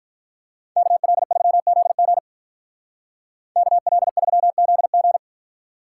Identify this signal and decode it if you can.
Morse code, heard as a keyed tone, about 43 wpm, 700 Hz. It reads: KL3ZG KL3ZG
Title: Speed Racing - USA 2x2 Call Signs 35wpm